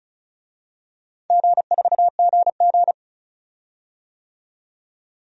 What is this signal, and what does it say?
Morse code, heard as a keyed tone, about 36 wpm, 690 Hz. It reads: G4GG
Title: DXCC-100 Call Signs - 4 Characters - 35wpm